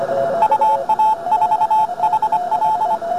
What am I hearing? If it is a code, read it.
Morse code, heard as a keyed tone, about 25 wpm, 870 Hz. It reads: UA4HL